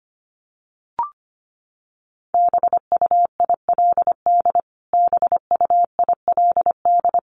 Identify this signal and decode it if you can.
Morse code, heard as a keyed tone, about 25 wpm, 700 Hz. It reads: BUILD BUILD
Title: Top 200 Words Repeated 25wpm